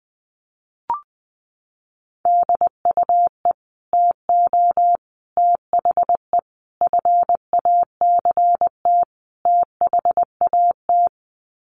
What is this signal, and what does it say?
Morse code, heard as a keyed tone, about 20 wpm, 700 Hz. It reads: DUE TO THE FACT THAT
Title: Sets of 5 Words Encore 20wpm